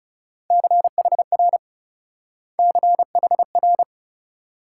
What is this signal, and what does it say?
Morse code, heard as a keyed tone, about 36 wpm, 700 Hz. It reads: CHR CHR
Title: Speed-Racing Multi-letter Phonemes - 30wpm